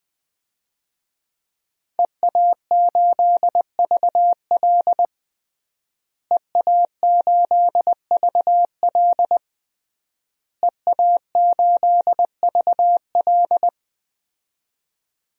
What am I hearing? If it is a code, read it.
Morse code, heard as a keyed tone, about 20 wpm, 700 Hz. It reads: EA8VL EA8VL EA8VL
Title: DXCC-100 Call Signs - 5 Characters - Repeated 3x - 20wpm